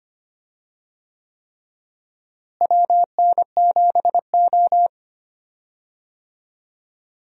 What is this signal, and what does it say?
Morse code, heard as a keyed tone, about 25 wpm, 700 Hz. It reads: WN7O
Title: USA Call Signs - 4 Characters - Repeated 3x - 25wpm